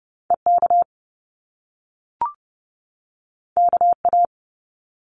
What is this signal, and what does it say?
Morse code, heard as a keyed tone, about 30 wpm, 700 Hz. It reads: EK KA
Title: Top Two Letter Combinations in Words 30wpm